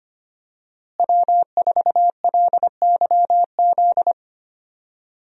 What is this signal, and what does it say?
Morse code, heard as a keyed tone, about 25 wpm, 700 Hz. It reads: W4LYZ
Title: USA Call Signs - 5 Characters - Repeated 3x - 25wpm